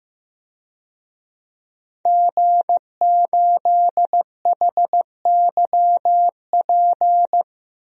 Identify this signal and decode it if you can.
Morse code, heard as a keyed tone, about 15 wpm, 700 Hz. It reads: G8HYP
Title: DXCC-100 Call Signs - 5 Characters - 15wpm